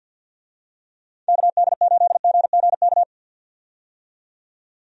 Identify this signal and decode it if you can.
Morse code, heard as a keyed tone, about 50 wpm, 700 Hz. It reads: KD8GGK
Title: USA Call Signs - 6 Characters - 50wpm